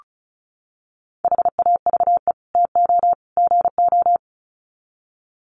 Morse code, heard as a keyed tone, about 34 wpm, 700 Hz. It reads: HAVE TO GO
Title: Sets of 3 Words 35wpm